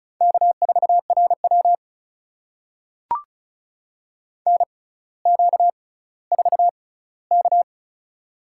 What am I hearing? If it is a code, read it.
Morse code, heard as a keyed tone, about 36 wpm, spaced as for 18 wpm, 700 Hz. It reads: K4RW NQ4K
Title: USA Call Signs - 4 Characters - 6x Character Spacing - 35wpm